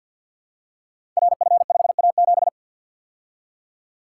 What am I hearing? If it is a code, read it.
Morse code, heard as a keyed tone, about 53 wpm, 700 Hz. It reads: RF5A7
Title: Code Groups - 5 Letter:Numbers - 50wpm